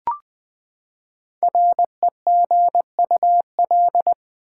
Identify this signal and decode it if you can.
Morse code, heard as a keyed tone, about 20 wpm, 700 Hz. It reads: REGUL
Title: Build A Word - 5 Letters At A Time - 10-Letter Words - 20wpm